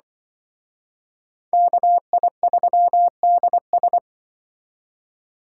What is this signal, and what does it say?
Morse code, heard as a keyed tone, about 24 wpm, 700 Hz. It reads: KI3DS
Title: Speed Racing - USA 2x2 Call Signs 20wpm